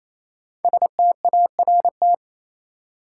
Morse code, heard as a keyed tone, about 28 wpm, 700 Hz. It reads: START
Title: Sentences from Top 300 Words - New Word Drill - 28wpm